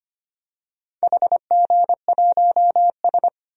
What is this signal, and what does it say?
Morse code, heard as a keyed tone, about 25 wpm, 700 Hz. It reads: HG1S
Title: DXCC-100 Call Signs - 4 Characters - Repeated 3x - 25wpm